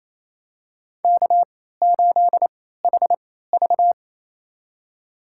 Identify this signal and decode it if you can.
Morse code, heard as a keyed tone, about 28 wpm, spaced as for 16 wpm, 700 Hz. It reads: K8HV
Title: USA Call Signs - 4 Characters - 3x Character Spacing - 28wpm